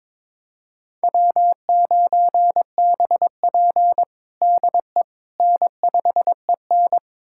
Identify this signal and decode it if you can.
Morse code, heard as a keyed tone, about 22 wpm, 700 Hz. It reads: W9BP DE N5EN
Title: USA Callsign DE Callsign - 4 Characters - 22wpm